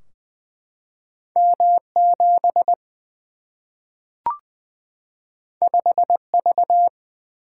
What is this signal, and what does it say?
Morse code, heard as a keyed tone, about 21 wpm, 700 Hz. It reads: M 7 5 V
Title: Single-Letter Single-Number 20wpm